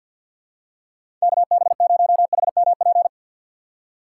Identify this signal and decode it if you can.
Morse code, heard as a keyed tone, about 51 wpm, 700 Hz. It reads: KB0HMP
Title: USA Call Signs - 6 Characters - 3x Character Spacing - 50wpm